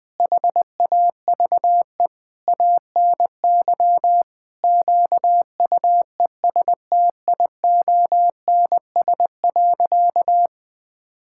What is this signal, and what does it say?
Morse code, heard as a keyed tone, about 20 wpm, 700 Hz. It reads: HAVE ANY QUESTIONS.
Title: Sentences with 8-Word Phrases - 20wpm